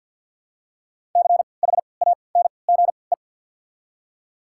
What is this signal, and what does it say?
Morse code, heard as a keyed tone, about 47 wpm, 700 Hz. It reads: C H A N G E
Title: 3x Character Spacing - Top 200 Words - 50wpm